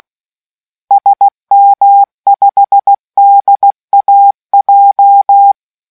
Morse code, heard as a keyed tone, about 16 wpm, 800 Hz. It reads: SM5DAJ